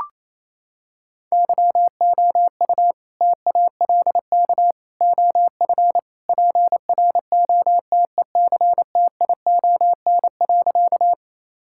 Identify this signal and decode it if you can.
Morse code, heard as a keyed tone, about 28 wpm, 700 Hz. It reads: YOU TALK OF PROTECTION.
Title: Sentences from Top 1600 Words 28wpm